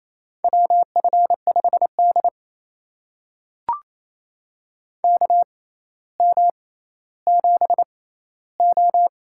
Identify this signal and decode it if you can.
Morse code, heard as a keyed tone, about 28 wpm, spaced as for 14 wpm, 700 Hz. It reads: WF5D KM7O
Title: USA Call Signs - 4 Characters - 6x Character Spacing - 28wpm